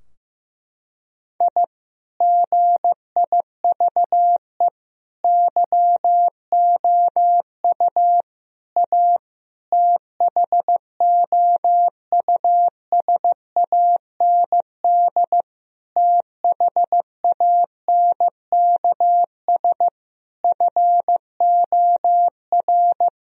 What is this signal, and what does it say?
Morse code, heard as a keyed tone, about 15 wpm, 700 Hz. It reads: I GIVE YOU A THOUSAND THANKS FOR
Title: Sentences from Top 1000 Words - Review Part 4 - 15wpm